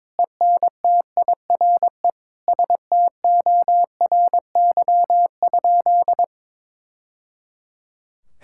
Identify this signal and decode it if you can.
Morse code, heard as a keyed tone, about 21 wpm, 700 Hz. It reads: ENTIRE STORY?
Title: Sentences from Top 1000 Words - Review Part 5 - 22wpm